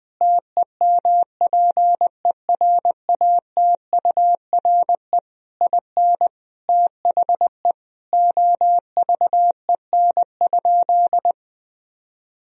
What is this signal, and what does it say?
Morse code, heard as a keyed tone, about 20 wpm, 700 Hz. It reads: TEMPERATURE IN THE OVEN?
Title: Sentences from Top 1000 Words - Review Part 4 - 20wpm